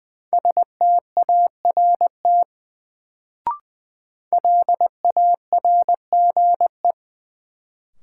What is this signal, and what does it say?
Morse code, heard as a keyed tone, about 20 wpm, 700 Hz. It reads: START LARGE
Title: Sets of 4 Words - New Word Drill - 20wpm